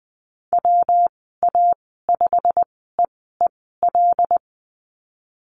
Morse code, heard as a keyed tone, about 20 wpm, spaced as for 15 wpm, 700 Hz. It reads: WA5EEL
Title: 161 Lesson 26g - Letter g - Call Signs 20wpm